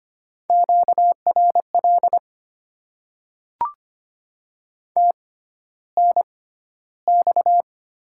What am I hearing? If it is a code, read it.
Morse code, heard as a keyed tone, about 25 wpm, spaced as for 13 wpm, 700 Hz. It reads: QRL TNX
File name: Top 20 QSO Elements - 6x Character Spacing - 25wpm